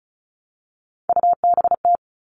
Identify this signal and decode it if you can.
Morse code, heard as a keyed tone, about 34 wpm, 700 Hz. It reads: UBT
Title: All Cut Numbers - 3-Digit 35wpm